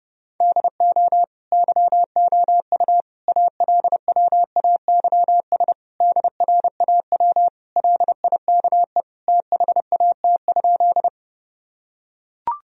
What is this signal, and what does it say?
Morse code, heard as a keyed tone, about 30 wpm, 700 Hz. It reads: DO YOU ALWAYS DRAW LIKE THAT?
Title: Sentences from Top 1000 Words - Review Part 1 - 30wpm